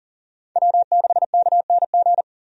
Morse code, heard as a keyed tone, about 40 wpm, 700 Hz. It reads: W6KNG
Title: USA Call Signs - 5 Characters - 40wpm